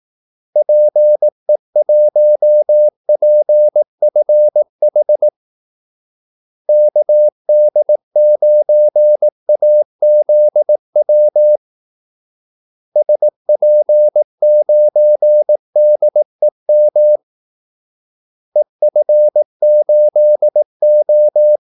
Morse code, heard as a keyed tone, about 18 wpm, 600 Hz. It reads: PE1PFH KD9AZW SP9DEM EF8O
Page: Kaldesignaler 18 wpm